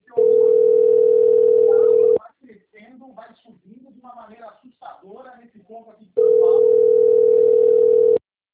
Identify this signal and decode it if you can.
Ringback tone (North America), 440 + 480 Hz, with speech.
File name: out- - - - - .34578